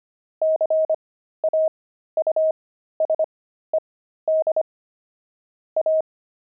Morse code, heard as a keyed tone, about 25 wpm, spaced as for 13 wpm, 630 Hz. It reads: CAUSED A
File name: pangram-fs-13-cs-25